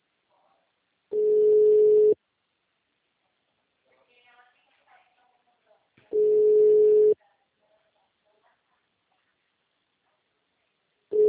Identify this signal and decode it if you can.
Ringback tone (Europe, 425 Hz).